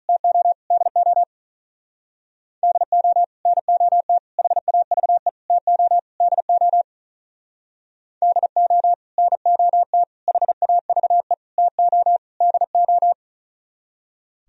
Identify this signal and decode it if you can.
Morse code, heard as a keyed tone, about 39 wpm, 700 Hz. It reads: TO DO DO NOT HAVE TO DO DO NOT HAVE TO DO
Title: Speed-Racing - Sets of 5 Words 35wpm